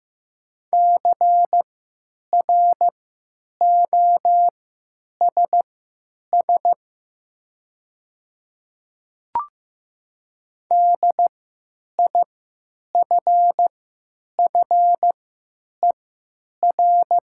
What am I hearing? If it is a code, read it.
Morse code, heard as a keyed tone, about 15 wpm, spaced as for 9 wpm, 700 Hz. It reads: CROSS DIFFER
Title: 3x Character Spacing - Top 500 Words - 15wpm